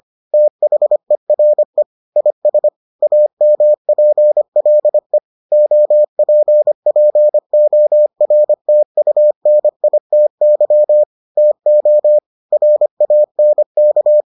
Morse code, heard as a keyed tone, about 25 wpm, 600 Hz. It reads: THERE IS AMPLE OPPORTUNITY TO RANK